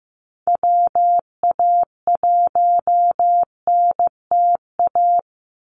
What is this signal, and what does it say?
Morse code, heard as a keyed tone, about 15 wpm, 700 Hz. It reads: WA1NTA
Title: Call Signs Repeated 3x - 15wpm